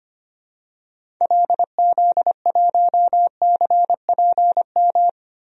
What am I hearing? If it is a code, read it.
Morse code, heard as a keyed tone, about 25 wpm, 700 Hz. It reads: LZ1CPM
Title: DXCC-100 Call Signs - 6 Characters - Repeated 3x - 25wpm